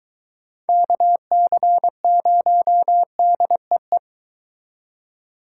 Morse code, heard as a keyed tone, about 23 wpm, 700 Hz. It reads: KC0DEE